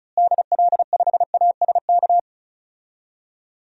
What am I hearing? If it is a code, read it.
Morse code, heard as a keyed tone, about 36 wpm, 700 Hz. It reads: DL5ASK
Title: DXCC-100 Call Signs - 6 Characters - Repeated 3x - 35wpm